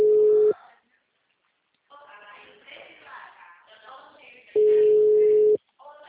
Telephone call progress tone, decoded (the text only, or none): ringback tone